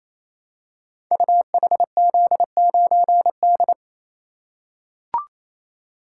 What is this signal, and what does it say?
Morse code, heard as a keyed tone, about 29 wpm, 700 Hz. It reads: UHZ9D
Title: Code Groups - 5 Letter:Numbers - 28wpm